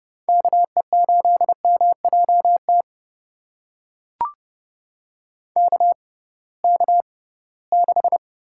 Morse code, heard as a keyed tone, about 30 wpm, spaced as for 19 wpm, 700 Hz. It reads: KE8MJT KK6
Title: USA Call Signs - 6 Characters - 6x Character Spacing - 30wpm